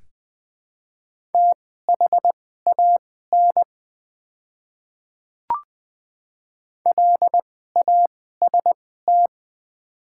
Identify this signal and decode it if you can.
Morse code, heard as a keyed tone, about 20 wpm, spaced as for 14 wpm, 700 Hz. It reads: THAN LAST
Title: 2x Character Spacing - Top 500 Words - 20wpm